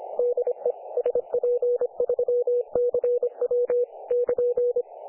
Morse code, heard as a keyed tone, about 25 wpm, 500 Hz. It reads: DE SP3CW <KN>
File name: dlazaawansowanych / nagraniazpasma / websdr recording 3564.7kHz